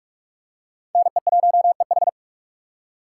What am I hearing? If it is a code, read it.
Morse code, heard as a keyed tone, about 44 wpm, 700 Hz. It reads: NE1EH
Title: USA Call Signs - 5 Characters - 6x Character Spacing - 45wpm